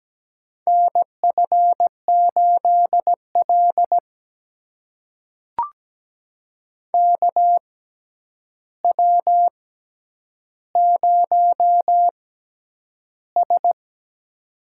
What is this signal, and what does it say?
Morse code, heard as a keyed tone, about 17 wpm, spaced as for 9 wpm, 700 Hz. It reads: NF8L KW0S
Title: USA Call Signs - 6x Character Spacing - 17wpm